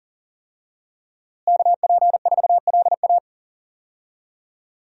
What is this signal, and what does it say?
Morse code, heard as a keyed tone, about 38 wpm, 700 Hz. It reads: KP4LA
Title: USA Call Signs - 5 Characters - Repeated 2x - 40wpm